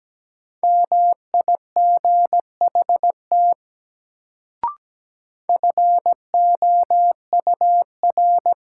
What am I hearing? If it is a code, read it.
Morse code, heard as a keyed tone, about 17 wpm, 700 Hz. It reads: MIGHT FOUR
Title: Sentences from Top 300 Words - New Word Drill - 17wpm